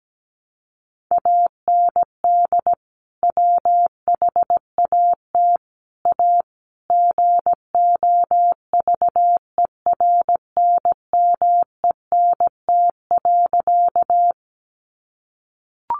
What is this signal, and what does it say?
Morse code, heard as a keyed tone, about 17 wpm, 700 Hz. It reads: AND WHAT A GOVERNMENT.
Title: Sentences from Top 600 Words 17wpm